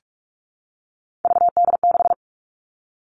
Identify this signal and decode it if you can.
Morse code, heard as a keyed tone, about 44 wpm, 700 Hz. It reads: VD6
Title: All Cut Numbers - 3-Digit 45wpm